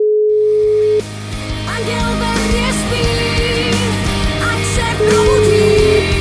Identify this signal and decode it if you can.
Ringback tone (Europe, 425 Hz), with noise.